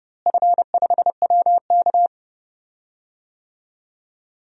Morse code, heard as a keyed tone, about 30 wpm, 700 Hz. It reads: F5WK
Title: DXCC-100 Call Signs - 4 Characters - Repeated 2x - 30wpm